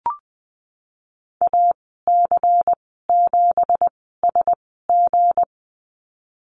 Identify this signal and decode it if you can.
Morse code, heard as a keyed tone, about 20 wpm, spaced as for 15 wpm, 700 Hz. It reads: AC7SG
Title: 179 Lesson 29g - Slash - Call Signs 20wpm